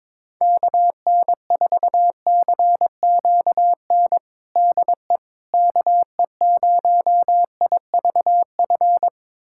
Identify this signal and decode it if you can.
Morse code, heard as a keyed tone, about 22 wpm, 700 Hz. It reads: KN4CQN DE KE0IVF
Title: USA Callsign DE Callsign - 6 Characters - 22wpm